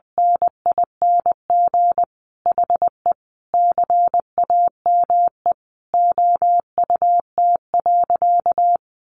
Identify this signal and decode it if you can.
Morse code, heard as a keyed tone, about 20 wpm, 700 Hz. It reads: NING HE CAME OUT.
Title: Sentences from Top 1300 Words 20wpm